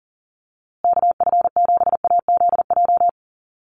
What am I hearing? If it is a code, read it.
Morse code, heard as a keyed tone, about 39 wpm, 700 Hz. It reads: KF7AZJ
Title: USA Call Signs Repeated 3x 40wpm